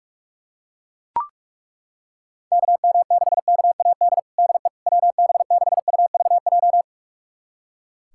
Morse code, heard as a keyed tone, about 41 wpm, 690 Hz. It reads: KM6KAD DE WB6UVJ
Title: USA Callsign DE Callsign - 6 Characters - 45wpm